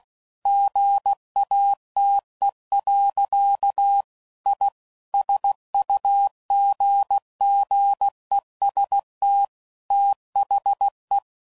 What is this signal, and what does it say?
Morse code, heard as a keyed tone, about 16 wpm, 800 Hz. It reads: GATE. I SUGGEST THE
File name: CW Code Groups @15WPM / Technical Subjects / AND Gate